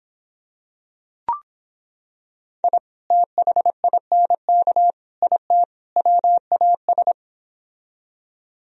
Morse code, heard as a keyed tone, about 26 wpm, 700 Hz. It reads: I THINK IT WAS